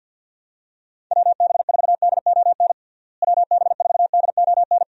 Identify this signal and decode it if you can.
Morse code, heard as a keyed tone, about 50 wpm, 700 Hz. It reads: WB4DON WB4DON
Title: USA Call Signs - 6 Characters - Repeated 2x - 50wpm